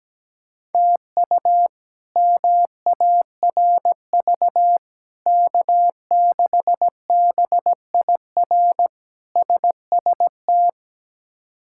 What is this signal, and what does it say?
Morse code, heard as a keyed tone, about 17 wpm, 700 Hz. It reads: TU MARV K6BIR SST
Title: K1USN SST QSO Practice - 17wpm